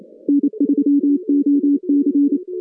Morse code, heard as a keyed tone, about 28 wpm, 280 Hz. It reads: N3OC
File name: contests / CW-SS-k5tr- -2100